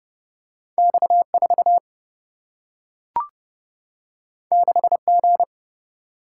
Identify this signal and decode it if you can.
Morse code, heard as a keyed tone, about 31 wpm, 700 Hz. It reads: X4 6G
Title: Single-Letter Single-Number 30wpm